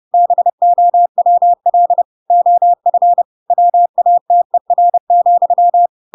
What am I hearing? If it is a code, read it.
Morse code, heard as a keyed tone, about 30 wpm, 700 Hz. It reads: BOWL OF WATER,
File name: Pinocchio / Pinocchio ch8 30 30 ik1hge